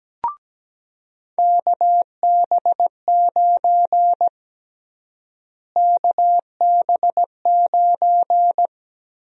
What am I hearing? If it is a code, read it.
Morse code, heard as a keyed tone, about 17 wpm, 700 Hz. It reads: KB9 KB9
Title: USA Call Sign Prefixes - Repeated 2x - 17wpm